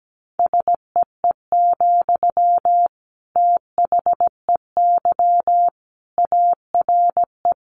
Morse code, heard as a keyed tone, about 17 wpm, 700 Hz. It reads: SEE, THEY ARE
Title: Sentences from Top 200 Words 17wpm